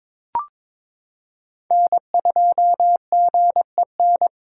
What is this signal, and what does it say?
Morse code, heard as a keyed tone, about 22 wpm, 700 Hz. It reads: N2GEN
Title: USA Callsign DE Callsign - 5 Characters - 22wpm